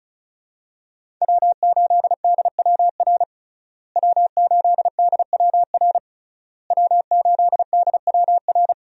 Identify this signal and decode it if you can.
Morse code, heard as a keyed tone, about 34 wpm, 700 Hz. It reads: W8DWR W8DWR W8DWR
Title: USA Call Signs - 5 Characters - Repeated 3x - 35wpm